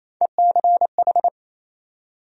Morse code, heard as a keyed tone, about 28 wpm, 700 Hz. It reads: ECH
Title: USA Call Sign Postfixes - 3 Letters 28wpm